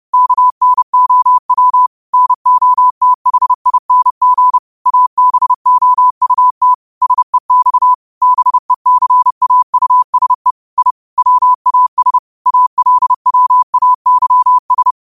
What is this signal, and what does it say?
Morse code, heard as a keyed tone, about 30 wpm, 1000 Hz. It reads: KNOW NOTHING ABOUT SEX BECAUSE I WAS ALWAYS